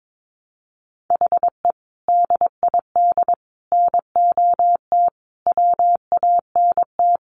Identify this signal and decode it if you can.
Morse code, heard as a keyed tone, about 22 wpm, 700 Hz. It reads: HE DID NOT WANT
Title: Sentences from Top 1300 Words 22wpm